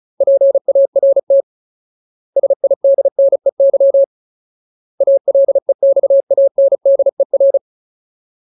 Morse code, heard as a keyed tone, about 35 wpm, 550 Hz. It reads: PART SIDNEY ALEXANDER
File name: Ultimate Word List / 35WPM Ultimate